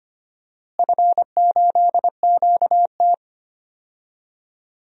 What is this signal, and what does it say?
Morse code, heard as a keyed tone, about 25 wpm, 700 Hz. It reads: F8QT
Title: DXCC-100 Call Signs - 4 Characters - Repeated 3x - 25wpm